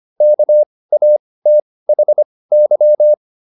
Morse code, heard as a keyed tone, about 25 wpm, spaced as for 18 wpm, 600 Hz. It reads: KATHY
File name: SS108 18